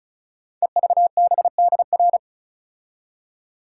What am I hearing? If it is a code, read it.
Morse code, heard as a keyed tone, about 36 wpm, 700 Hz. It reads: EVBDR
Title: Code Groups - 5 Letter:Numbers - 35wpm